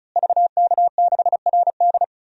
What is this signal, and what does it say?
Morse code, heard as a keyed tone, about 36 wpm, 700 Hz. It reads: VK6RD
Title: DXCC-100 Call Signs - 5 Characters - Repeated 3x - 35wpm